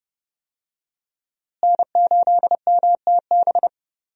Morse code, heard as a keyed tone, about 28 wpm, 700 Hz. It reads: N8MTB
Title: USA Call Signs - 5 Characters - Repeated 3x - 30wpm